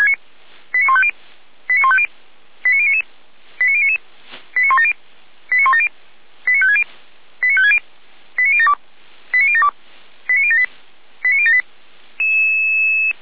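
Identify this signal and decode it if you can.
Five-tone selective call (ZVEI), heard as quick runs of five tones, tones of about 70 ms. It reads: 89150 89150 89099 89099 89170 89170 89560 89560 89072 89072 89078 89078
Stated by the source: ZVEI five-tone selective call received by SDR, recorded by Feuerwehr Orlamünde.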